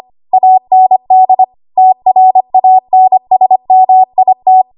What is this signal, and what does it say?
Morse code, heard as a keyed tone, about 25 wpm, 750 Hz. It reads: AND TRANSMIT
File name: Morse / 25WPM